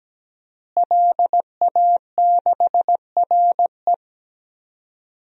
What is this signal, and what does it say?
Morse code, heard as a keyed tone, about 17 wpm, 700 Hz. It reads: LA6RE
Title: DXCC-100 Call Signs - 5 Characters - Repeated 3x - 17wpm